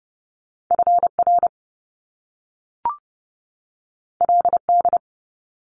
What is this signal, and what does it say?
Morse code, heard as a keyed tone, about 30 wpm, 700 Hz. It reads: FR LD
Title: Top Two Letter Combinations in Words 30wpm